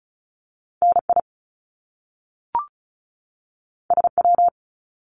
Morse code, heard as a keyed tone, about 36 wpm, 700 Hz. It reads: N I S W
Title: Top Two Letter Combinations in Words 35wpm